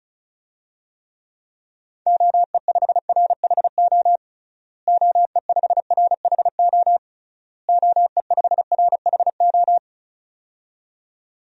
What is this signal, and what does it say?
Morse code, heard as a keyed tone, about 36 wpm, 700 Hz. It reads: OE5RHO OE5RHO OE5RHO
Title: DXCC-100 Call Signs - 6 Characters - Repeated 3x - 35wpm